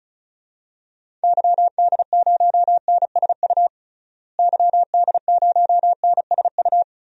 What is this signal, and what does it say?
Morse code, heard as a keyed tone, about 36 wpm, 700 Hz. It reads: YD0NSU YD0NSU
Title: DXCC-100 Call Signs - 6 Characters - Repeated 3x - 35wpm